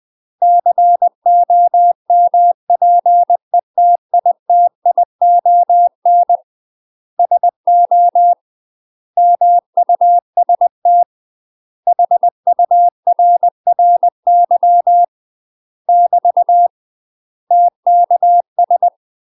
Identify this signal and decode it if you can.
Morse code, heard as a keyed tone, about 20 wpm, 700 Hz. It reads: COMPETITION SO MUST HURRY <BT> TKS
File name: UK MASTER QSOs TRAINING / 20WPM / UK MASTER QSOs 0062